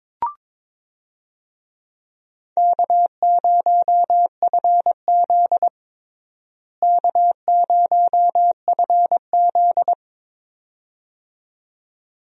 Morse code, heard as a keyed tone, about 22 wpm, 700 Hz. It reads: K0FZ K0FZ
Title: USA Call Signs - 4 Characters - Repeated 2x - 22wpm